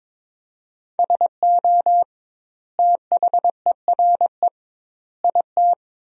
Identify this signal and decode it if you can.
Morse code, heard as a keyed tone, about 22 wpm, 700 Hz. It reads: SO THERE IT
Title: 2x Word-Spacing - Sentences from Top 100 Words - 22wpm